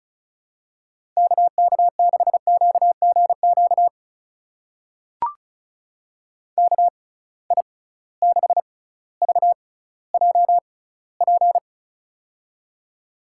Morse code, heard as a keyed tone, about 36 wpm, spaced as for 18 wpm, 700 Hz. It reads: KK6QGQ KI6VJP
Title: USA Call Signs - 6 Characters - 6x Character Spacing - 35wpm